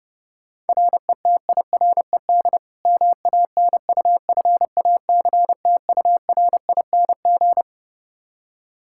Morse code, heard as a keyed tone, about 30 wpm, 700 Hz. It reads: RETIRED MANUFACTURING
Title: Practice QSO - Exchange 3 - More Info - 30wpm